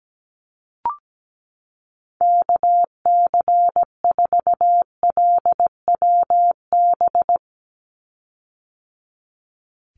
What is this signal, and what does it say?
Morse code, heard as a keyed tone, about 17 wpm, 700 Hz. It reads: KC4LWB
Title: USA Call Signs 17wpm